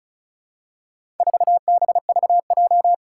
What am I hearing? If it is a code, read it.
Morse code, heard as a keyed tone, about 36 wpm, 700 Hz. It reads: 4BVJ
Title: Four Challenging Letter-Number Combinations 35wpm